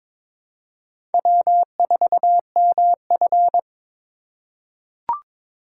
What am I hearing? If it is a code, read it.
Morse code, heard as a keyed tone, about 22 wpm, 700 Hz. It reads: W4MF
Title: USA Call Signs - 8x Character Spacing - 22wpm